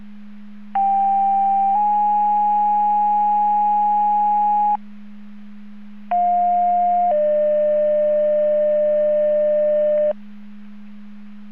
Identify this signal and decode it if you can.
Two-tone page, heard as a one-second tone, then a three-second tone, about 790 then 830 Hz, about 710 then 600 Hz. It is heard from 0.7 to 10.1 s.
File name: Portage 154.385 Paging